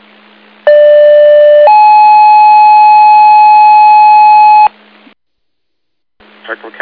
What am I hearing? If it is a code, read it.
Two-tone page, heard as a one-second tone, then a three-second tone, about 590 then 850 Hz, with speech. It is heard from 0.7 to 4.7 s.